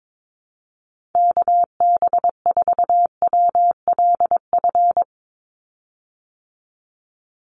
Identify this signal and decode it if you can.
Morse code, heard as a keyed tone, about 22 wpm, 700 Hz. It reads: KB4WLF
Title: Call Signs Repeated 3x - 22wpm